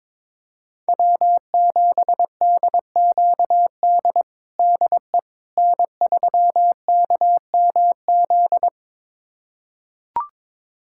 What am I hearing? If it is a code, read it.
Morse code, heard as a keyed tone, about 22 wpm, 700 Hz. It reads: W7DQD DE N3KMZ
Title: USA Callsign DE Callsign - 5 Characters - 22wpm